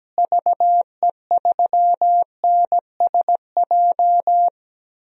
Morse code, heard as a keyed tone, about 17 wpm, 700 Hz. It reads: VE3NSJ
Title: DXCC-100 Call Signs - 6 Characters - Repeated 3x - 17wpm